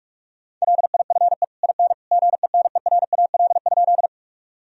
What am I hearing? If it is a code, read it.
Morse code, heard as a keyed tone, about 47 wpm, 700 Hz. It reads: LIFE IN GENERAL?
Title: Sentences from Top 1000 Words - Review Part 3 - 45wpm